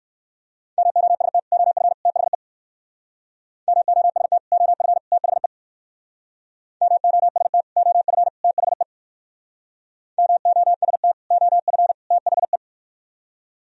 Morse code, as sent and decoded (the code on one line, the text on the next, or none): -- --- ... - / --- ..-. / - .... / -- --- ... - / --- ..-. / - .... / -- --- ... - / --- ..-. / - .... / -- --- ... - / --- ..-. / - .... .
MOST OF THE MOST OF THE MOST OF THE MOST OF THE